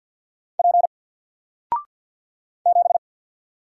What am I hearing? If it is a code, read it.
Morse code, heard as a keyed tone, about 48 wpm, 700 Hz. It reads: P 7
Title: Single Letter-Number Alt 50wpm